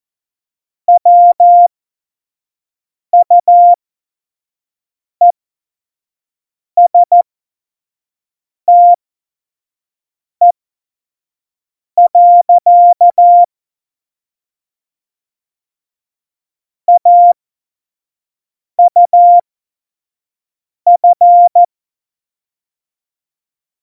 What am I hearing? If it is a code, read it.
Morse code, heard as a keyed tone, about 14 wpm, spaced as for 5 wpm, 700 Hz. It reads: WUESTE. AUF
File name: Morsekurs / Witze / Witz 26. September 17, Tempo 25